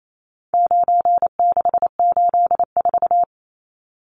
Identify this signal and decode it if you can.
Morse code, heard as a keyed tone, about 28 wpm, 700 Hz. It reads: 9684
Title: Numbers - 4-Digit 28wpm